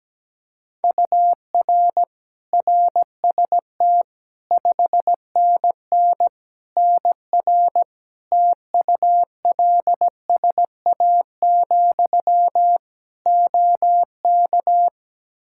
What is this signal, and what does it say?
Morse code, heard as a keyed tone, about 17 wpm, 700 Hz. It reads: UR RST 5NN NR TULSA, OK
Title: Practice QSO - Exchange 1 - Beginning - 17wpm